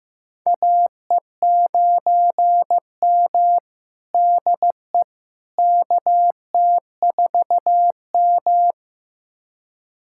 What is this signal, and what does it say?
Morse code, heard as a keyed tone, about 15 wpm, 700 Hz. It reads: AE9M DE KT4M
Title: USA Callsign DE Callsign - 4 Characters - 15wpm